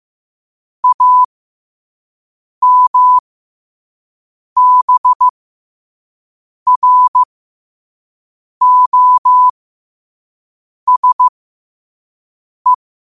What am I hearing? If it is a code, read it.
Morse code, heard as a keyed tone, about 15 wpm, spaced as for 5 wpm, 1000 Hz. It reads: AMBROSE
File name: qotdfiles / 5wpm